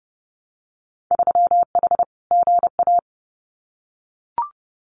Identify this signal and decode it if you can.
Morse code, heard as a keyed tone, about 30 wpm, 700 Hz. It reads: 3H GA
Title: Winter Field Day Exchanges Encore 30wpm